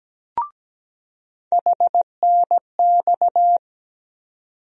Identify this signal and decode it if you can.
Morse code, heard as a keyed tone, about 17 wpm, 700 Hz. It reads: HNX
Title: USA Call Sign Postfixes - 3 Letters 17wpm